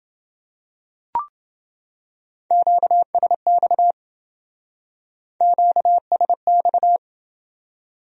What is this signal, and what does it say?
Morse code, heard as a keyed tone, about 29 wpm, 700 Hz. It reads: QSX QSX